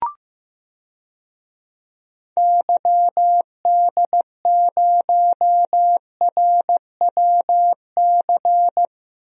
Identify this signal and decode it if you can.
Morse code, heard as a keyed tone, about 15 wpm, 700 Hz. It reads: YD0RWC
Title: DXCC-100 Call Signs - 6 Characters - Repeated 2x - 15wpm